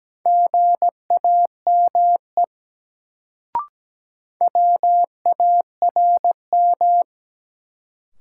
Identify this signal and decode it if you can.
Morse code, heard as a keyed tone, about 17 wpm, 700 Hz. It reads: GAME WARM
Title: Sentences from Top 500 Words - New Word Drill - 17wpm